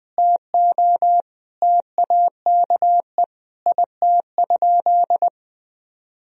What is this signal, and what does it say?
Morse code, heard as a keyed tone, about 21 wpm, 700 Hz. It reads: TO TAKE IT?
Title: Sentences from Top 300 Words and Abbreviations 20wpm